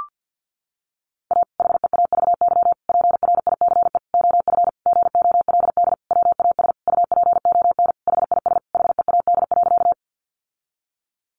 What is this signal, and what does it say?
Morse code, heard as a keyed tone, about 50 wpm, 700 Hz. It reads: A HEAVY PRICE OF GOLD WAS UPON HIS HEAD.
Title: Sentences from Top 1100 Words 50wpm